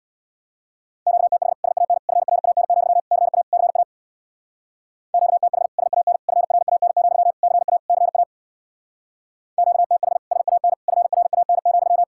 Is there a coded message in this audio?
Yes: Morse code from 1.1 to 12.0 s, heard as a keyed tone, about 64 wpm, 700 Hz. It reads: QTH SAN FRAN, CA CA QTH SAN FRAN, CA CA QTH SAN FRAN,